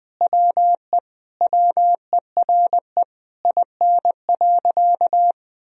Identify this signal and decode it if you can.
Morse code, heard as a keyed tone, about 20 wpm, 700 Hz. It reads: WE WERE IN.
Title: Sentences from Top 1000 Words - Review Part 3 - 20wpm